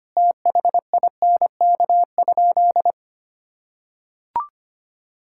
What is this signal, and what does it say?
Morse code, heard as a keyed tone, about 25 wpm, 700 Hz. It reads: THINK?
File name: practice / 25wpm / Sentences with 3-Word Phrases - 25wpm